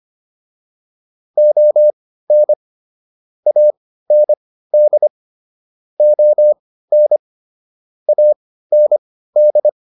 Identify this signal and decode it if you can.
Morse code, heard as a keyed tone, about 25 wpm, spaced as for 15 wpm, 600 Hz. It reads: ON AND ON AND